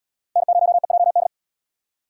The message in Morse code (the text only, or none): N0EJD